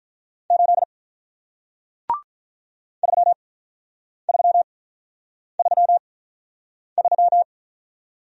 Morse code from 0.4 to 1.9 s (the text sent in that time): 8